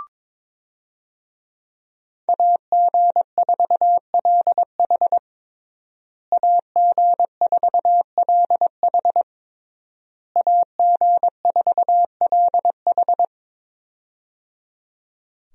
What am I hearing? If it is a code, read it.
Morse code, heard as a keyed tone, about 22 wpm, 700 Hz. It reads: AG4LH AG4LH AG4LH
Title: USA Call Signs - 5 Characters - Repeated 3x - 22wpm